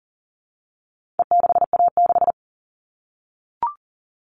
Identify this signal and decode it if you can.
Morse code, heard as a keyed tone, about 40 wpm, 700 Hz. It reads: E6A6
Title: All Cut Numbers - 4-Digit 40wpm